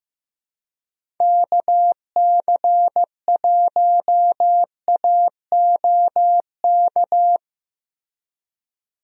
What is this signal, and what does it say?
Morse code, heard as a keyed tone, about 15 wpm, 700 Hz. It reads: KC1AOK
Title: USA Call Signs - 6 Characters - Repeated 2x - 15wpm